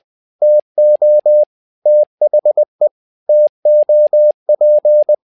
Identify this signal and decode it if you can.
Morse code, heard as a keyed tone, about 20 wpm, 600 Hz. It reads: TO THE TOP
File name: learning center / tower trouble / tower trouble 20wpm